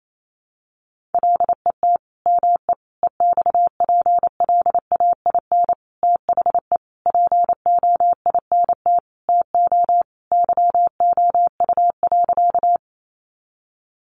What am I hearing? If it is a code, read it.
Morse code, heard as a keyed tone, about 28 wpm, 700 Hz. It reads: LET ME EXPLAIN THE POINT TO YOU.
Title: Sentences from Top 1100 Words 28wpm